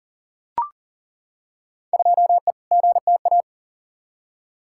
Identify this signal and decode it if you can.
Morse code, heard as a keyed tone, about 40 wpm, 700 Hz. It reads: 2E GTA
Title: Field Day Exchanges Encore - 40wpm